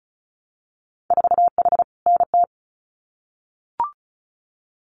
Morse code, heard as a keyed tone, about 36 wpm, 700 Hz. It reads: 4H NT
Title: Winter Field Day Exchanges Encore 35wpm